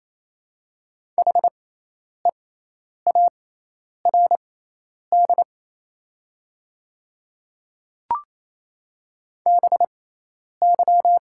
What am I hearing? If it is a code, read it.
Morse code, heard as a keyed tone, about 28 wpm, spaced as for 10 wpm, 700 Hz. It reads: HEARD BY
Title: 6x Character Spacing - Top 500 Words - 28wpm